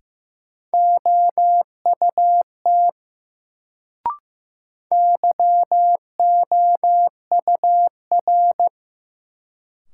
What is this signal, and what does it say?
Morse code, heard as a keyed tone, about 15 wpm, 700 Hz. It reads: OUT YOUR
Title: Sets of 2 Words - New Word Drill - 15wpm